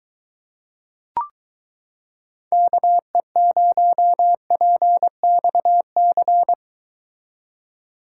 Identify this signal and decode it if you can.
Morse code, heard as a keyed tone, about 24 wpm, 700 Hz. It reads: KE0PXC